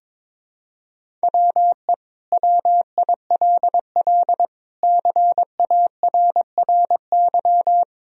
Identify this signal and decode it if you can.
Morse code, heard as a keyed tone, about 22 wpm, 700 Hz. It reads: WE WILL CARRY
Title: Sentences from Top 1000 Words - Review Part 1 - 22wpm